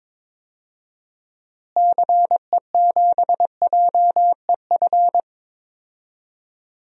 Morse code, heard as a keyed tone, about 22 wpm, 700 Hz. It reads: CE7JEF
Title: DXCC-100 Call Signs - 6 Characters - Repeated 2x - 22wpm